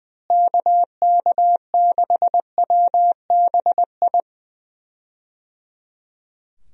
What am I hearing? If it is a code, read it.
Morse code, heard as a keyed tone, about 21 wpm, 700 Hz. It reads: KK6WBI
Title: USA Call Signs - 6 Characters - Repeated 3x - 20wpm